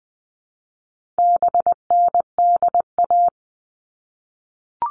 Morse code, heard as a keyed tone, about 20 wpm, 700 Hz. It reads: BNDA
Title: All Cut Numbers - 4-Digit 20wpm